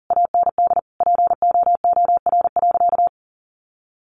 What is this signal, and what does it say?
Morse code, heard as a keyed tone, about 40 wpm, 700 Hz. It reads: AND POOR.
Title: Sentences from Top 1100 Words 40wpm